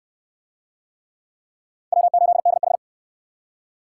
Morse code, heard as a keyed tone, about 57 wpm, 700 Hz. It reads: W8DH